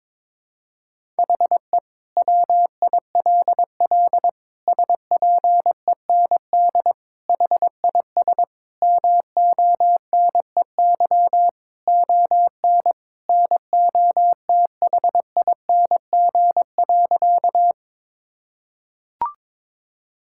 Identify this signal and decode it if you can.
Morse code, heard as a keyed tone, about 21 wpm, 700 Hz. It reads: HE WILL SPEND HIS MONEY ON NOTHING.
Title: Sentences from Top 1000 Words - Review Part 5 - 22wpm